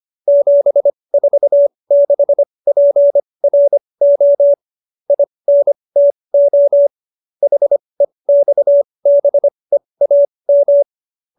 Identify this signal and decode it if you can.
Morse code, heard as a keyed tone, about 25 wpm, spaced as for 20 wpm, 570 Hz. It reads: 746PRO INTO HEXBEAM